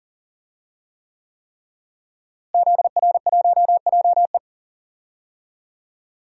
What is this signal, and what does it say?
Morse code, heard as a keyed tone, about 40 wpm, 700 Hz. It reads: ZR1JE
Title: DXCC-100 Call Signs - 5 Characters - Repeated 3x - 40wpm